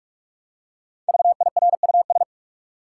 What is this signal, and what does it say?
Morse code, heard as a keyed tone, about 48 wpm, 700 Hz. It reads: VIRUS
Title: Five Letter Words - 45wpm